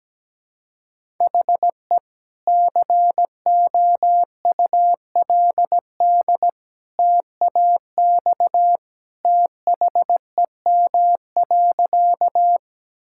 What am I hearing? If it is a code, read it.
Morse code, heard as a keyed tone, about 17 wpm, 700 Hz. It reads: HE COULD TAX THEM.
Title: Sentences from Top 1000 Words - Review Part 5 - 17wpm